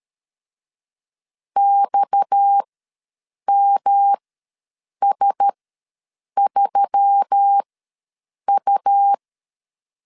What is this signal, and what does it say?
Morse code, heard as a keyed tone, about 13 wpm, spaced as for 7 wpm, 800 Hz. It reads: XMS3U
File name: LEZIONI / 7 GF9Z / 7 GF9 ws 12wpm 3344